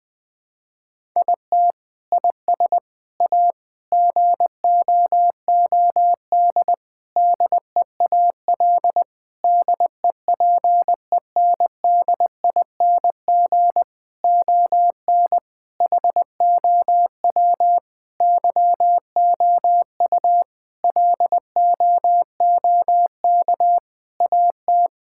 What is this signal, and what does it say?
Morse code, heard as a keyed tone, about 21 wpm, 700 Hz. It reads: IT IS A GOOD DEAL DEPENDING ON HOW YOU LOOK AT
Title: Sentences with 7-Word Phrases - 20wpm